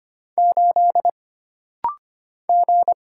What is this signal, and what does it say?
Morse code, heard as a keyed tone, about 25 wpm, 700 Hz. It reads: 8 G
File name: Single Character (Letter:Number:Punctuation:Prosign) - Rapid-Fire - 25wpm